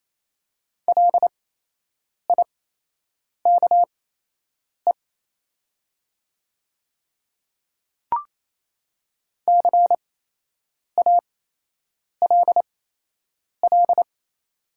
Morse code, heard as a keyed tone, about 28 wpm, spaced as for 8 wpm, 700 Hz. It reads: LIKE CALL
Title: 8x Character Spacing - Top 100 Words - 28wpm